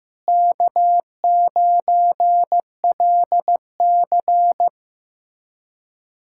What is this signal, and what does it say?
Morse code, heard as a keyed tone, about 15 wpm, 700 Hz. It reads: K9LC
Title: K1USN SST QSO Practice - 15wpm